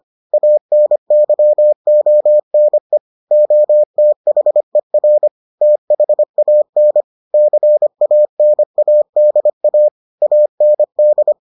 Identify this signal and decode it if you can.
Morse code, heard as a keyed tone, about 25 wpm, 600 Hz. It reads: ANYONE OTHER THAN CANADA AND